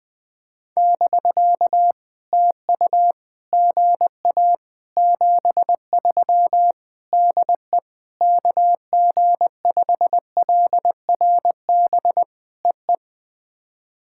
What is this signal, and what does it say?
Morse code, heard as a keyed tone, about 19 wpm, 700 Hz. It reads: <BK> TU GA 73 DE KG5LRB EE
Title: POTA QSO Practice - 20wpm